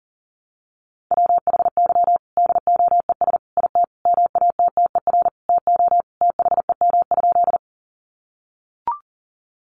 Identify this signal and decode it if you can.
Morse code, heard as a keyed tone, about 37 wpm, 700 Hz. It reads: WHY DOES IT MATTER TO THEM?
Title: Sentences from Top 700 Words 40wpm